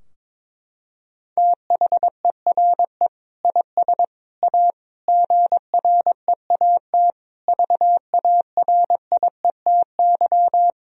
Morse code, heard as a keyed tone, about 22 wpm, 700 Hz. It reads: THERE IS A GREAT VARIETY
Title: Sentences from Top 1000 Words - Review Part 4 - 22wpm